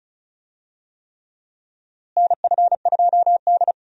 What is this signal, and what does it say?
Morse code, heard as a keyed tone, about 34 wpm, 700 Hz. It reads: NF2D
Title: USA Call Signs - 4 Characters - Repeated 3x - 35wpm